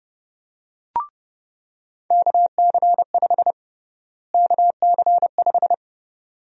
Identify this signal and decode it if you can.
Morse code, heard as a keyed tone, about 30 wpm, 700 Hz. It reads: KC5 KC5
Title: USA Call Sign Prefixes - Repeated 2x - 30wpm